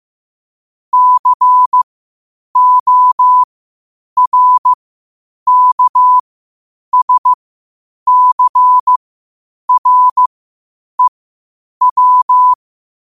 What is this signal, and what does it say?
Morse code, heard as a keyed tone, about 15 wpm, spaced as for 8 wpm, 1000 Hz. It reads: CORKSCREW
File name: qotdfiles / 8wpm